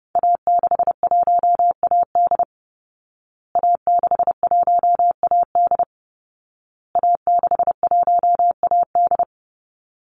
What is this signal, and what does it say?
Morse code, heard as a keyed tone, about 30 wpm, 700 Hz. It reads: A61AD A61AD A61AD
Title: Call Signs Repeated 2x - 30wpm